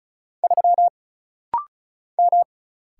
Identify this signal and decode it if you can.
Morse code, heard as a keyed tone, about 36 wpm, 700 Hz. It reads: 3 M
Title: Single Letter-Number Alt - Rapid-Fire 35wpm